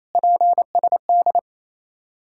Morse code, heard as a keyed tone, about 28 wpm, 690 Hz. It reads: PSD
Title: Ontario QSO Party Exchange 28wpm